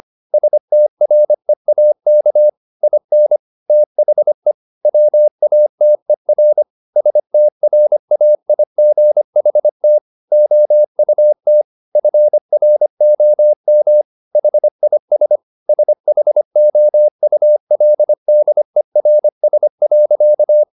Morse code, heard as a keyed tone, about 25 wpm, 600 Hz. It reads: STREAK IN THE WATER STRAIGHT OUT FROM HIS SHOULDERS.
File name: the old man and the sea part 1 25wpm